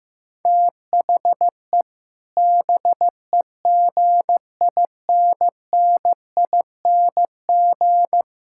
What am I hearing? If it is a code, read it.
Morse code, heard as a keyed tone, about 15 wpm, 700 Hz. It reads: THE BEGINNING
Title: Sentences from Top 1000 Words - Review Part 6 - 15wpm